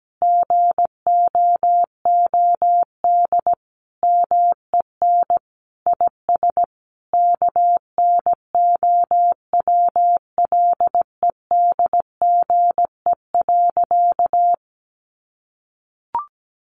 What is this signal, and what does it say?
Morse code, heard as a keyed tone, about 17 wpm, 700 Hz. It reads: GOOD MEN IS KNOWLEDGE.
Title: Sentences from Top 1100 Words 17wpm